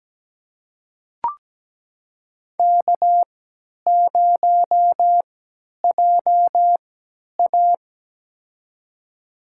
Morse code, heard as a keyed tone, about 17 wpm, spaced as for 10 wpm, 700 Hz. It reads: K0JA
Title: USA Call Signs - 3x Character Spacing - 17wpm